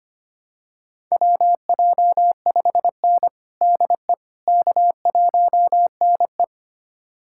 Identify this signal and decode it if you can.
Morse code, heard as a keyed tone, about 25 wpm, 700 Hz. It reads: WJ5N DE K1NE
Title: USA Callsign DE Callsign - 4 Characters - 25wpm